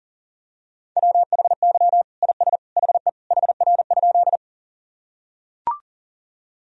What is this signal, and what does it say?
Morse code, heard as a keyed tone, about 40 wpm, 700 Hz. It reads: WHY IS HE HR?
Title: Sentences from Top 200 Words and Abbreviations 40wpm